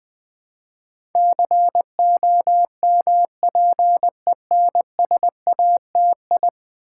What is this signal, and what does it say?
Morse code, heard as a keyed tone, about 20 wpm, 700 Hz. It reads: COMPENSATI
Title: Build A Word - 5 Letters At A Time - 12-Letter Words - 20wpm